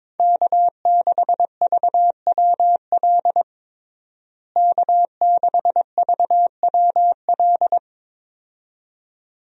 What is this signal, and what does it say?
Morse code, heard as a keyed tone, about 22 wpm, 700 Hz. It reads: K6VWL K6VWL
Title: USA Call Signs - 5 Characters - Repeated 2x - 22wpm